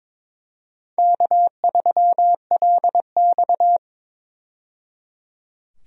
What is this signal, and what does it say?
Morse code, heard as a keyed tone, about 22 wpm, 700 Hz. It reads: K3LX
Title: USA Call Signs - 4 Characters - 22wpm